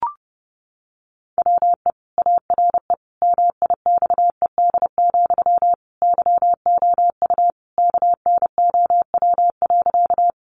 Morse code, heard as a keyed tone, about 30 wpm, 700 Hz. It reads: WE ARE MIXED, YOU KNOW.
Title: Sentences from Top 1500 Words 30wpm